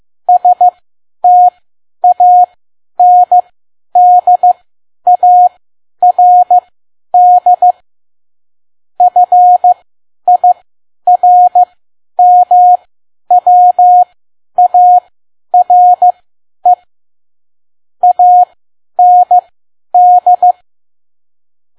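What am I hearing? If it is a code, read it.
Morse code, heard as a keyed tone, about 15 wpm, spaced as for 10 wpm, 720 Hz. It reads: STANDARD FIRMWARE AND
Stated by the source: Morse code practice at 10 WPM.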